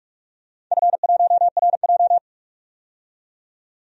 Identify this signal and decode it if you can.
Morse code, heard as a keyed tone, about 41 wpm, 700 Hz. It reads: F1RJ
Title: DXCC-100 Call Signs - 4 Characters - 45wpm